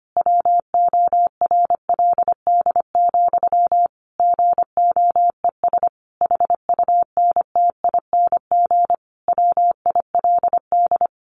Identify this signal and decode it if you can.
Morse code, heard as a keyed tone, about 25 wpm, 700 Hz. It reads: WORLD, GOES HUNTING WILD
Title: A Sound Of Thunder 25wpm